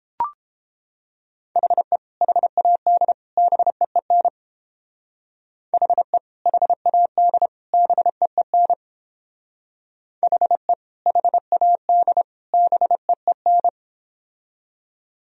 Morse code, heard as a keyed tone, about 29 wpm, 700 Hz. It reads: HE HAD BEEN HE HAD BEEN HE HAD BEEN